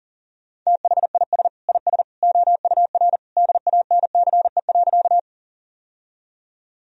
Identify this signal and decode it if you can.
Morse code, heard as a keyed tone, about 40 wpm, 700 Hz. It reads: THIS IS OUR DANCE.
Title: Sentences from Top 1000 Words - Review Part 4 - 40wpm